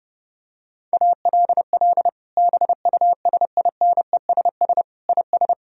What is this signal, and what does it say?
Morse code, heard as a keyed tone, about 30 wpm, 700 Hz. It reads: ALL BUSINESS IS
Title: Sentences from Top 1000 Words - Review Part 3 - 30wpm